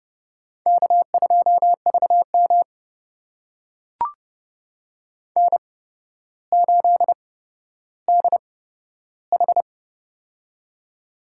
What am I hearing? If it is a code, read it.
Morse code, heard as a keyed tone, about 30 wpm, spaced as for 13 wpm, 700 Hz. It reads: K2VM N8DH
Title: USA Call Signs - 4 Characters - 8x Character Spacing - 30wpm